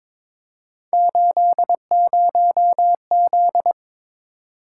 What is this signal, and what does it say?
Morse code, heard as a keyed tone, about 22 wpm, 700 Hz. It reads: 80Z
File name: practice / 22wpm / Three Letter-Number Combinations - 22wpm